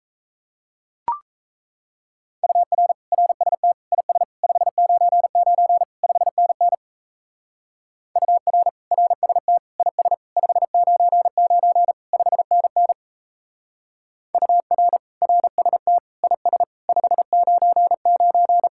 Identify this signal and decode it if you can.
Morse code, heard as a keyed tone, about 33 wpm, 700 Hz. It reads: UR RST IS 599 5NN UR RST IS 599 5NN UR RST IS 599
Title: Speed-Racing Common QSO Phrases - 28wpm